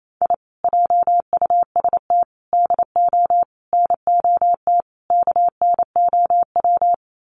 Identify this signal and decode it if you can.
Morse code, heard as a keyed tone, about 28 wpm, 700 Hz. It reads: I JUST DO NOT KNOW
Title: Sets of 5 Words Encore 28wpm